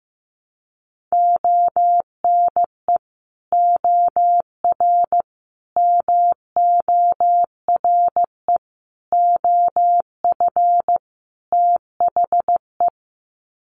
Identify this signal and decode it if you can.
Morse code, heard as a keyed tone, about 15 wpm, 700 Hz. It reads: ONE OR MORE OF THE
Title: Sets of 5 Words Encore 15wpm